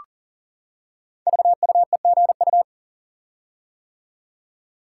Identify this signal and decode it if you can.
Morse code, heard as a keyed tone, about 38 wpm, 690 Hz. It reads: VUEGU
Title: Code Groups - 5 Letters - 40wpm